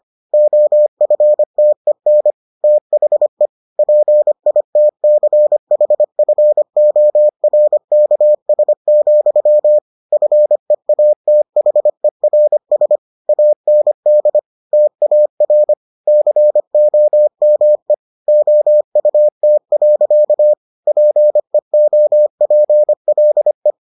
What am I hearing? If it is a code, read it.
Morse code, heard as a keyed tone, about 25 wpm, 600 Hz. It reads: OFTEN THE PITCHFORKS, FEATHERS AND TAR COME OUT. PEOPLE